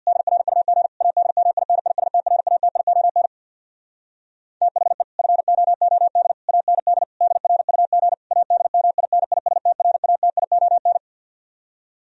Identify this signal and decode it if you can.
Morse code, heard as a keyed tone, about 64 wpm, 700 Hz. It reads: DRUG ADMINISTRATION THE FOOD AND DRUG ADMINISTRATION